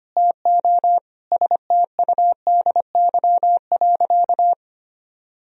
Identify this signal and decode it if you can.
Morse code, heard as a keyed tone, about 25 wpm, 700 Hz. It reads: TO STUDY.
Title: Sentences from Top 1000 Words - Review Part 1 - 25wpm